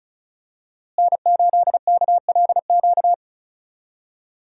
Morse code, heard as a keyed tone, about 36 wpm, 700 Hz. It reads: N8KLQ
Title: USA Call Signs - 5 Characters - Repeated 3x - 35wpm